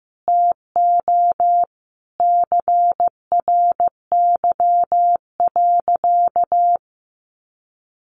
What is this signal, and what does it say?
Morse code, heard as a keyed tone, about 15 wpm, 700 Hz. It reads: TO CRY.
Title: Sentences from Top 500 Words 15wpm